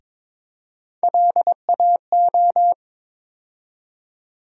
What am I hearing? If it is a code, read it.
Morse code, heard as a keyed tone, about 22 wpm, 700 Hz. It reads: LAO
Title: USA Call Sign Postfixes - 3 Letters 22wpm